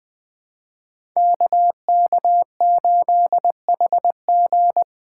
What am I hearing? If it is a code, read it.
Morse code, heard as a keyed tone, about 20 wpm, 700 Hz. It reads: KK8HG
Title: Code Groups - 5 Letter:Numbers - 20wpm